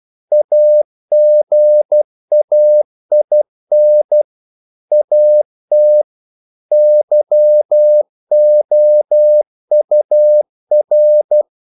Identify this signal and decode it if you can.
Morse code, heard as a keyed tone, about 12 wpm, 600 Hz. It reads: AGAIN AT YOUR
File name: CW / Practice / If12